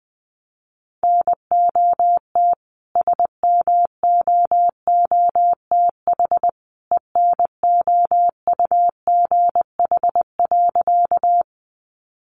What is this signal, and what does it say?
Morse code, heard as a keyed tone, about 20 wpm, 700 Hz. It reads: NOT SMOOTH ENOUGH.
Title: Sentences from Top 1400 Words 20wpm